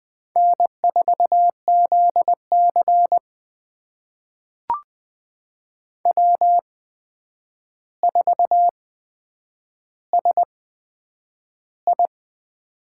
Morse code, heard as a keyed tone, about 20 wpm, spaced as for 9 wpm, 700 Hz. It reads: N4ZC W4SI
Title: USA Call Signs - 4 Characters - 8x Character Spacing - 20wpm